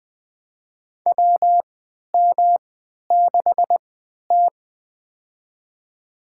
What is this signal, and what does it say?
Morse code, heard as a keyed tone, about 21 wpm, spaced as for 11 wpm, 700 Hz. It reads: WM6T
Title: USA Call Signs - 3x Character Spacing - 20wpm